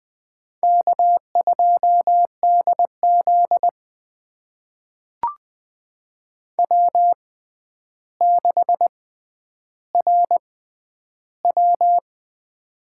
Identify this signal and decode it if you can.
Morse code, heard as a keyed tone, about 20 wpm, spaced as for 10 wpm, 700 Hz. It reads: K2DZ W6RW
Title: USA Call Signs - 4 Characters - 6x Character Spacing - 20wpm